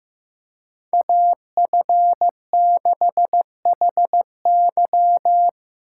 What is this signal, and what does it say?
Morse code, heard as a keyed tone, about 15 wpm, 700 Hz. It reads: AF6HY
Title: USA Call Signs - 3x Character Spacing - 15wpm